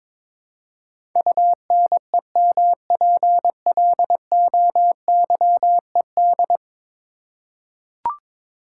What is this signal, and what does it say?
Morse code, heard as a keyed tone, about 22 wpm, 700 Hz. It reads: UNEMPLOYED
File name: practice / 22wpm / 10 Letter Words - First 3 Letters Given - 22wpm